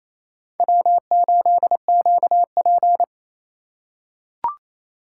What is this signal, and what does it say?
Morse code, heard as a keyed tone, about 27 wpm, 700 Hz. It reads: W8QP
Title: USA Call Signs - 4 Characters - 4x Character Spacing - 28wpm